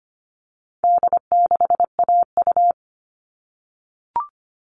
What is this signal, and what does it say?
Morse code, heard as a keyed tone, about 25 wpm, 700 Hz. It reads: D6AU
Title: All Cut Numbers - 4-Digit 25wpm